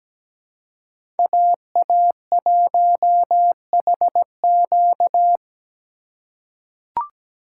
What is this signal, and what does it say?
Morse code, heard as a keyed tone, about 17 wpm, 700 Hz. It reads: AA1HQ
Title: USA Call Signs - 5 Characters - 2x Character Spacing - 17wpm